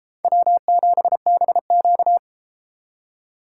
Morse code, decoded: W7BQ

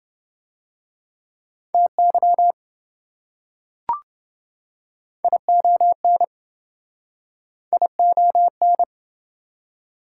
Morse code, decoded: TY ION ION

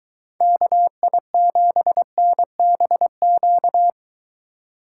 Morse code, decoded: KI7NBQ

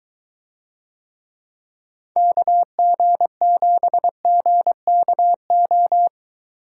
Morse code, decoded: KG7GKO

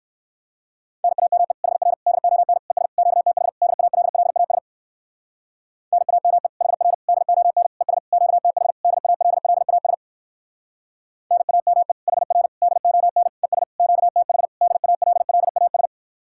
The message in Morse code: NAME HR DON ES QTH DALLAS NAME HR DON ES QTH DALLAS NAME HR DON ES QTH DALLAS

